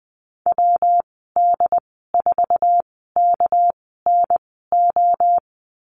Morse code, heard as a keyed tone, about 20 wpm, spaced as for 15 wpm, 700 Hz. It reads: WD4KNO
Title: 238 Lesson 38g - Letter q - Call Signs 20wpm